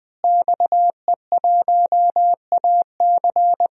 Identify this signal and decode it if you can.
Morse code, heard as a keyed tone, about 20 wpm, 700 Hz. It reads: XE1AC
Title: DXCC-100 Call Signs - 5 Characters - Repeated 3x - 20wpm